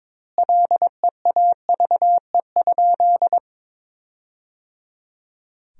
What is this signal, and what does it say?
Morse code, heard as a keyed tone, about 22 wpm, 700 Hz. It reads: LEAVE?
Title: 2x Word-Spacing - Sentences from Top 400 Words - 22wpm